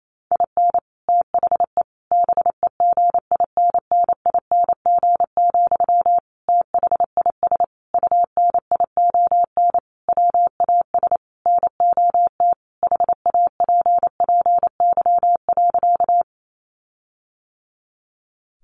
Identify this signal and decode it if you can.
Morse code, heard as a keyed tone, about 28 wpm, 700 Hz. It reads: IN THE BEGINNING, THIS UNION WAS NOT HAPPY.
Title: Sentences from Top 1400 Words 28wpm